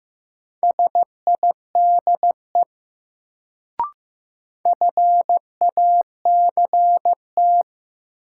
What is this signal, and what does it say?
Morse code, heard as a keyed tone, about 15 wpm, 700 Hz. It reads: SIDE FACT
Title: Sets of 2 Words - New Word Drill - 15wpm